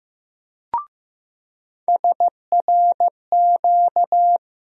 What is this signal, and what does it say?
Morse code, heard as a keyed tone, about 15 wpm, 700 Hz. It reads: SRQ
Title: USA Call Sign Postfixes - 3 Letters 15wpm